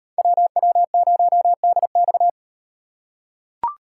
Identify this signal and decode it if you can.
Morse code, heard as a keyed tone, about 36 wpm, 700 Hz. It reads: WW0DX